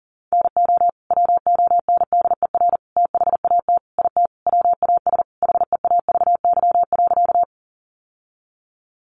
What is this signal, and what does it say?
Morse code, heard as a keyed tone, about 40 wpm, 700 Hz. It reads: NO WONDER THAT IT WAS HEAVY.
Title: Sentences from Top 1100 Words 40wpm